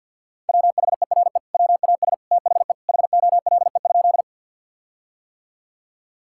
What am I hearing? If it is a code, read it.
Morse code, heard as a keyed tone, about 50 wpm, 700 Hz. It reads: WHERE WAS THE HOLE?
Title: Sentences from Top 1600 Words 50wpm